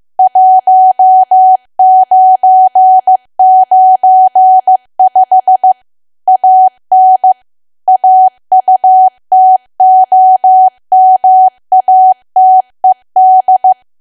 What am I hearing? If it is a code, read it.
Morse code, heard as a keyed tone, about 15 wpm, 750 Hz. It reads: 1995 AN AUTOMATED
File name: cwa / W1AW-Level2 / 15WPM